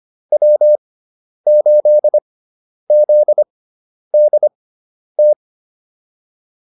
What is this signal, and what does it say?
Morse code, heard as a keyed tone, about 25 wpm, spaced as for 10 wpm, 600 Hz. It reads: W8ZDT
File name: QSO108 10